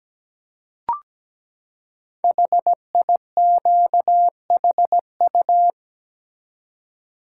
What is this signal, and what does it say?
Morse code, heard as a keyed tone, about 17 wpm, 700 Hz. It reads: HIQHU
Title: Code Groups - 5 Letters - 17wpm